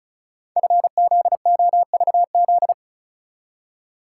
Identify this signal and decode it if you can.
Morse code, heard as a keyed tone, about 36 wpm, 690 Hz. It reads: FZOVZ
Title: Code Groups - 5 Letters - 35wpm